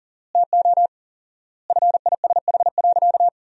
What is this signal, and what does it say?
Morse code, heard as a keyed tone, about 40 wpm, 700 Hz. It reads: TO FISH.
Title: 4x Word-Spacing - Sentences from Top 300 Words - 40wpm